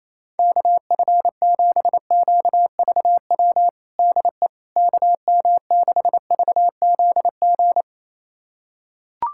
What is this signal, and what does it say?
Morse code, heard as a keyed tone, about 28 wpm, 700 Hz. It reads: KF7QVW DE KM6VZG
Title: USA Callsign DE Callsign 28wpm